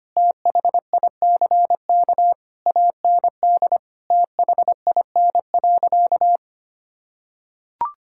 Morse code, heard as a keyed tone, about 25 wpm, 700 Hz. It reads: THICK AND THIN.
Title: Sentences from Top 1600 Words 25wpm